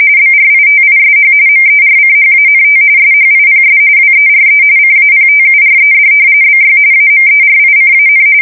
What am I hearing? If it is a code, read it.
RTTY, about 45.45 baud. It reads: BRITISH MERCHANT SHIPS WERE CAPTURED BY AMERICAN
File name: fsk170 / usnavy1